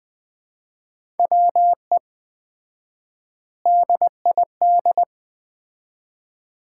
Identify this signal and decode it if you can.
Morse code, heard as a keyed tone, about 20 wpm, 700 Hz. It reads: WE DID
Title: 4x Word-Spacing - Sets of 4 Words - 20wpm